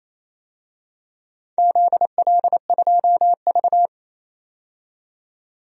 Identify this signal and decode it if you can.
Morse code, heard as a keyed tone, about 28 wpm, 700 Hz. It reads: ZL2V
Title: DXCC-100 Call Signs - 4 Characters - 28wpm